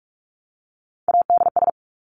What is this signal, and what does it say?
Morse code, heard as a keyed tone, about 46 wpm, 700 Hz. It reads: ADS
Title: Abbreviations 45wpm